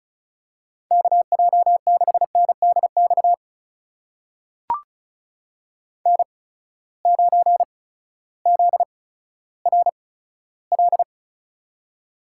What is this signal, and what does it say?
Morse code, heard as a keyed tone, about 36 wpm, spaced as for 16 wpm, 700 Hz. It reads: KJ6NDX N9ZRL
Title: USA Call Signs - 8x Character Spacing - 35wpm